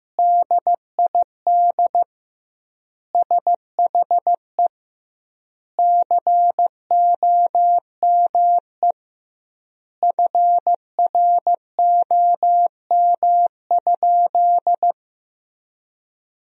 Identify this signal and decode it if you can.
Morse code, heard as a keyed tone, about 15 wpm, 700 Hz. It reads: DID SHE COME FROM?
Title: 2x Word-Spacing - Sentences from Top 200 Words - 15wpm